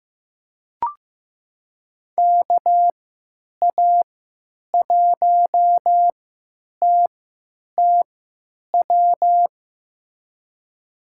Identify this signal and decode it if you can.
Morse code, heard as a keyed tone, about 16 wpm, spaced as for 9 wpm, 700 Hz. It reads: KA1TTW
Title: USA Call Signs - 6 Characters - 3x Character Spacing - 15wpm